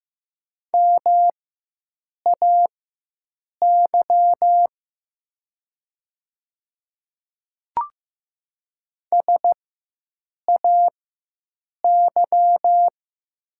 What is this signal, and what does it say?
Morse code, heard as a keyed tone, about 14 wpm, spaced as for 7 wpm, 700 Hz. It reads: MAY SAY
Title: 4x Character Spacing - Top 500 Words - 15wpm